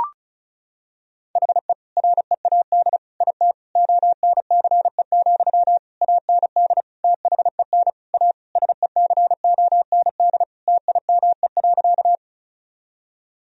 Morse code, heard as a keyed tone, about 36 wpm, 700 Hz. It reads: HE READ IT ONCE, AND THEN A SECOND TIME.
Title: Sentences from Top 1000 Words - Review Part 1 - 35wpm